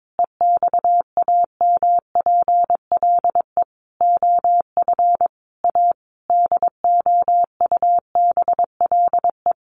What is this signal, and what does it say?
Morse code, heard as a keyed tone, about 22 wpm, 700 Hz. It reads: EXAMPLE OF A DOUBLE
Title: Sentences from Top 1500 Words 22wpm